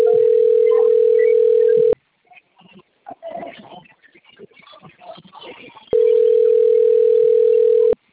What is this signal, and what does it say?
Ringback tone (North America), 440 + 480 Hz.